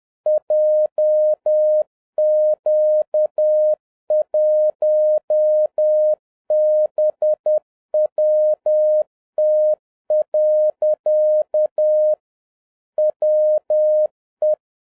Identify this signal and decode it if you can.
Morse code, heard as a keyed tone, about 10 wpm, 610 Hz. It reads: JQ1BWT. WE
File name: 2011-07-mp3 / n46 JOHN K3TN 10wpm